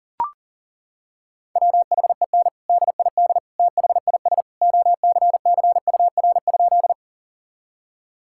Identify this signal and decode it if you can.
Morse code, heard as a keyed tone, about 40 wpm, 700 Hz. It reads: WHEN DID THIS OCCUR?
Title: Sentences from Top 1000 Words - Review Part 6 - 40wpm